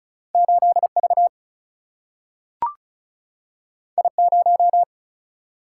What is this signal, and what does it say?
Morse code, heard as a keyed tone, about 36 wpm, 700 Hz. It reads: 8 V I 0
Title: Single-Letter Single-Number 35wpm